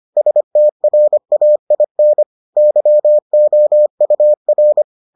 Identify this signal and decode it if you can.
Morse code, heard as a keyed tone, about 25 wpm, 600 Hz. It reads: STRAIN YOUR